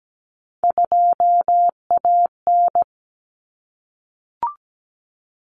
Morse code, heard as a keyed tone, about 17 wpm, 700 Hz. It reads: 2AN
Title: Common Cut Numbers - 3-Digit 17wpm